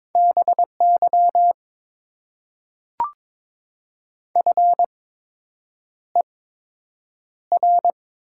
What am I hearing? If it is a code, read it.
Morse code, heard as a keyed tone, about 22 wpm, spaced as for 8 wpm, 700 Hz. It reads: BY FER
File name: Top 90 QSO Elements - 8x Character Spacing - 22wpm